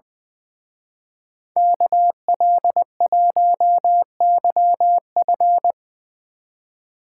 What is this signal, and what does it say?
Morse code, heard as a keyed tone, about 20 wpm, 700 Hz. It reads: KL1YF
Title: Speed Racing - USA Call Signs 20wpm